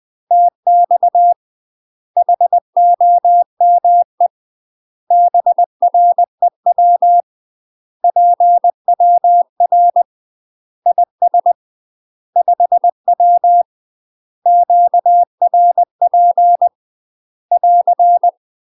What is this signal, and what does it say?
Morse code, heard as a keyed tone, about 20 wpm, 700 Hz. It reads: TX HOME BREW PWR IS 5W QRP <AR>
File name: UK MASTER QSOs TRAINING/20WPM